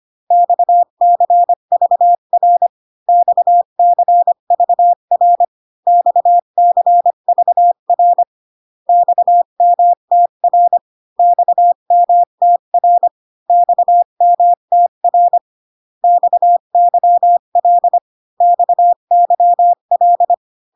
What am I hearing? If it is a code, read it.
Morse code, heard as a keyed tone, about 25 wpm, 700 Hz. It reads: XCVR XCVR XCVR XMTR XMTR XMTR XYL XYL